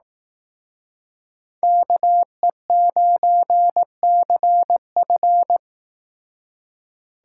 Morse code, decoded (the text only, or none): KE9CF